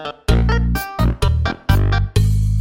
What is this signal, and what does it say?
There is no code.